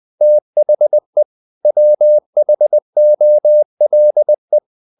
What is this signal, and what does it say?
Morse code, heard as a keyed tone, about 20 wpm, 600 Hz. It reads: THE WHOLE
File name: the old man and the sea part 2 20wpm